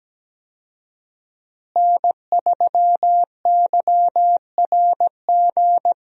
Morse code, heard as a keyed tone, about 17 wpm, 700 Hz. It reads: N3YRG
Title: USA Call Signs - 5 Characters - Repeated 3x - 17wpm